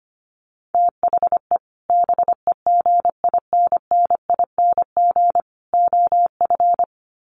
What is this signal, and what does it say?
Morse code, heard as a keyed tone, about 25 wpm, 700 Hz. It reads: THE BEGINNING OF
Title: Sets of 3 Words 25wpm